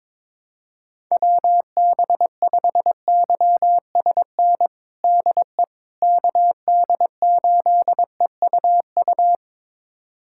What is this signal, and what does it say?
Morse code, heard as a keyed tone, about 22 wpm, 700 Hz. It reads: WB5YSN DE KD8EUU
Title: USA Callsign DE Callsign - 6 Characters - 22wpm